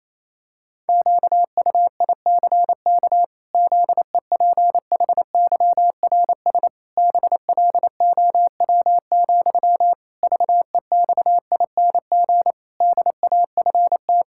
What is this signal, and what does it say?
Morse code, heard as a keyed tone, about 28 wpm, 700 Hz. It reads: QUICK ZEPHYRS BLOW, VEXING DAFT
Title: Pangram Challenge v2 - 28wpm